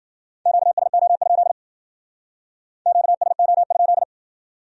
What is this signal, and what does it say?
Morse code, heard as a keyed tone, about 49 wpm, 700 Hz. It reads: QSO? QSO?
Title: Speed-Racing Q-Codes - 40wpm